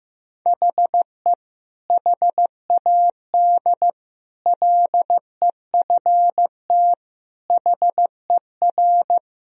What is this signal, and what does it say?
Morse code, heard as a keyed tone, about 15 wpm, 700 Hz. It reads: HE HAD LEFT HER
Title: Sentences from Top 1000 Words - Review Part 3 - 15wpm